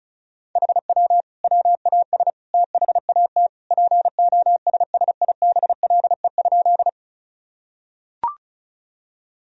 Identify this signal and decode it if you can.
Morse code, heard as a keyed tone, about 36 wpm, 700 Hz. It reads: HW WAS THAT POSSIBLE?
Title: Sentences from Top 500 Words and Abbreviations 35wpm